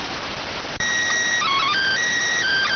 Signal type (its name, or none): SSTV (slow-scan television)